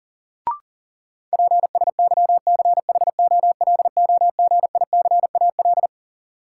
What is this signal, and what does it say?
Morse code, heard as a keyed tone, about 40 wpm, 700 Hz. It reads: PSYCHOLOGICAL
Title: Sentences from Top 1400 Words - New Word Drill - 40wpm